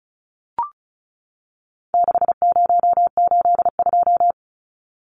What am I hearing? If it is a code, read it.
Morse code, heard as a keyed tone, about 34 wpm, 700 Hz. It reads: 6082
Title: Numbers - 4-Digit 35wpm